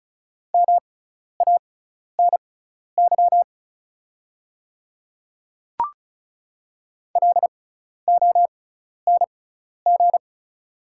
Morse code, heard as a keyed tone, about 34 wpm, spaced as for 12 wpm, 700 Hz. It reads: MANY LONG
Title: 6x Character Spacing - Top 200 Words - 35wpm